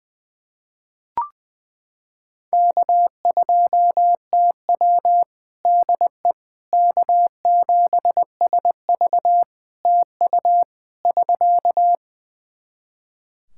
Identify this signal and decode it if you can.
Morse code, heard as a keyed tone, about 21 wpm, 700 Hz. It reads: K2TW DE K7SV TU <SK>
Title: K1USN SST QSO Practice 20wpm